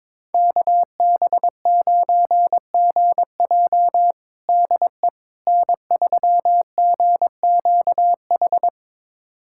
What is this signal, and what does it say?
Morse code, heard as a keyed tone, about 22 wpm, 700 Hz. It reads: KB9GJ DE N3GQH
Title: USA Callsign DE Callsign - 5 Characters - 22wpm